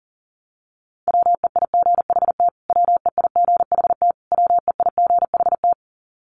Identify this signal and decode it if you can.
Morse code, heard as a keyed tone, about 40 wpm, 700 Hz. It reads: WEIGHT WEIGHT WEIGHT
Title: Top 500 Words Repeated 40wpm